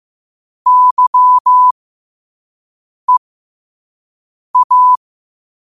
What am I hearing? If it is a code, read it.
Morse code, heard as a keyed tone, about 15 wpm, spaced as for 5 wpm, 1000 Hz. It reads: YEA
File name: qotdfiles / 5wpm